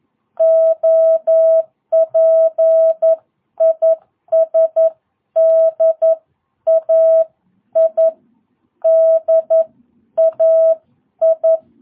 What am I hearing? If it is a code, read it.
Morse code, heard as a keyed tone, about 11 wpm, spaced as for 9 wpm, 650 Hz. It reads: OP IS DAI DAI
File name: jh8cbh / 2022cw